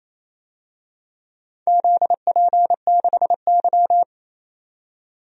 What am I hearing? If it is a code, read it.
Morse code, heard as a keyed tone, about 28 wpm, 700 Hz. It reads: ZP6Y
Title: DXCC-100 Call Signs - 4 Characters - 28wpm